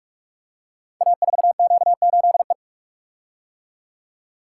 Morse code, heard as a keyed tone, about 44 wpm, 700 Hz. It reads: A4Q8E
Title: Code Groups - 5 Letter:Numbers - 45wpm